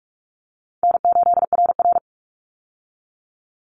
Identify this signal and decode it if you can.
Morse code, heard as a keyed tone, about 44 wpm, 700 Hz. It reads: N8RR
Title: Call Signs Repeated 2x - 45wpm